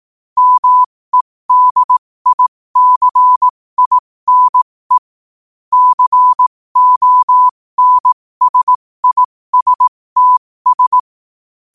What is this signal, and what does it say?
Morse code, heard as a keyed tone, about 18 wpm, spaced as for 15 wpm, 1000 Hz. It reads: MEDICINE CONSISTS
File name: qotdfiles / 15wpm